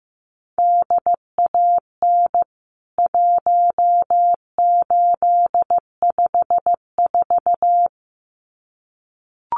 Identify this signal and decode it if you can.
Morse code, heard as a keyed tone, about 15 wpm, 700 Hz. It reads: DAN 1854
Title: CWT Exchange 15wpm